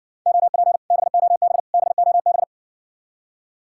Morse code, heard as a keyed tone, about 61 wpm, 690 Hz. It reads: OP BOB BOB